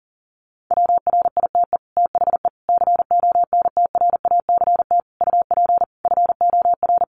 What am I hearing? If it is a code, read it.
Morse code, heard as a keyed tone, about 40 wpm, 700 Hz. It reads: WRITE THE CONTRACT UP FOR
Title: Sentences from Top 1400 Words 40wpm